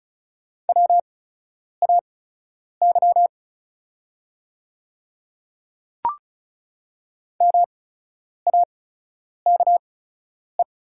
Morse code, heard as a keyed tone, about 36 wpm, spaced as for 10 wpm, 700 Hz. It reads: WAY MAKE
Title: 8x Character Spacing - Top 100 Words - 35wpm